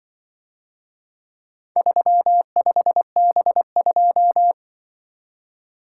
Morse code, heard as a keyed tone, about 25 wpm, 700 Hz. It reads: 35B2
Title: Speed-Racing - Four Letter-Number Combinations - 20wpm